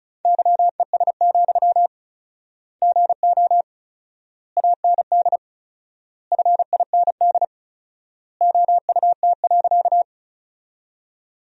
Morse code, heard as a keyed tone, about 36 wpm, 700 Hz. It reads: YES, GO AND FIND OUT.
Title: 4x Word-Spacing - Sentences from Top 500 Words - 35wpm